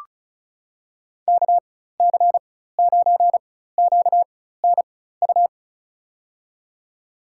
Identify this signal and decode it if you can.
Morse code, heard as a keyed tone, about 36 wpm, spaced as for 16 wpm, 700 Hz. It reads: KC9QNU
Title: USA Call Signs - 4x Character Spacing - 35wpm